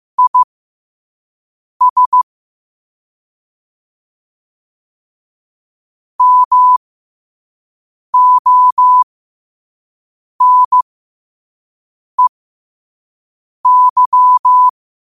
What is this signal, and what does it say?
Morse code, heard as a keyed tone, about 15 wpm, spaced as for 5 wpm, 1000 Hz. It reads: IS MONEY
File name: qotdfiles / 5wpm